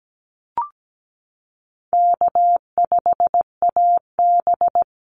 Morse code, heard as a keyed tone, about 17 wpm, 700 Hz. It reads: K5AB
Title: Easy Call Signs Repeated 2x - 17wpm